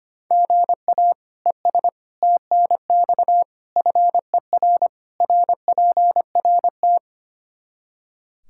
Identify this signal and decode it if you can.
Morse code, heard as a keyed tone, about 25 wpm, 700 Hz. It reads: GA ES TNX FER RPRT
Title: Practice QSO - Exchange 1 - Beginning - 25wpm